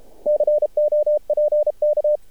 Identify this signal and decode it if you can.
Morse code, heard as a keyed tone, about 32 wpm, 600 Hz. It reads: COPK
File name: archived / CRSnet